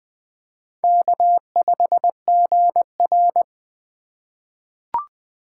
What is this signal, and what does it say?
Morse code, heard as a keyed tone, about 20 wpm, 700 Hz. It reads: K5GR
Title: USA Call Signs - 4x Character Spacing - 20wpm